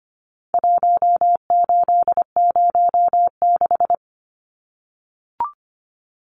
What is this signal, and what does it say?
Morse code, heard as a keyed tone, about 25 wpm, 700 Hz. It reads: 1806
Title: Numbers - 4-Digit 25wpm